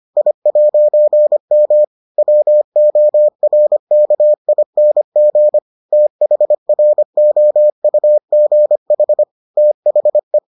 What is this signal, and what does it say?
Morse code, heard as a keyed tone, about 25 wpm, 600 Hz. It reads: I'M WORKING THROUGH THE